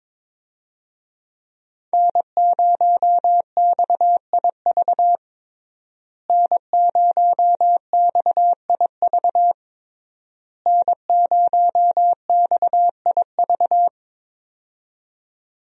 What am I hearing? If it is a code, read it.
Morse code, heard as a keyed tone, about 22 wpm, 700 Hz. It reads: N0XIV N0XIV N0XIV
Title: USA Call Signs - 5 Characters - Repeated 3x - 22wpm